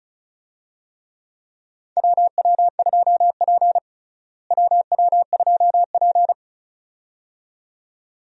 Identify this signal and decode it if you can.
Morse code, heard as a keyed tone, about 34 wpm, 700 Hz. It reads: WW2P WW2P
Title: USA Call Signs - 4 Characters - Repeated 2x - 35wpm